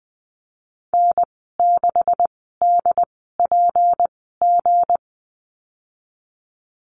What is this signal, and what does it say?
Morse code, heard as a keyed tone, about 19 wpm, spaced as for 14 wpm, 700 Hz. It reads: N6DPG
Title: 161 Lesson 26g - Letter g - Call Signs 20wpm